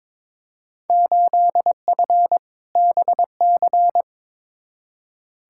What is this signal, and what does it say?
Morse code, heard as a keyed tone, about 22 wpm, 700 Hz. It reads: 8F BC
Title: Field Day Exchanges Encore - 22wpm